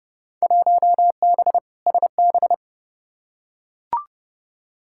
Morse code, heard as a keyed tone, about 32 wpm, 700 Hz. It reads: 1B SB
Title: Field Day Exchanges - 30wpm